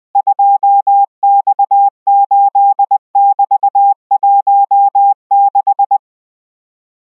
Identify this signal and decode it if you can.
Morse code, heard as a keyed tone, about 20 wpm, 800 Hz. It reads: 2X8<BT>16